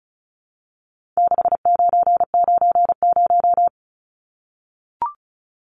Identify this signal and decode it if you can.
Morse code, heard as a keyed tone, about 34 wpm, 700 Hz. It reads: 6990
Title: Numbers - 4-Digit 35wpm